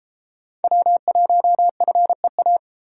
Morse code, heard as a keyed tone, about 31 wpm, 700 Hz. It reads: W1FEA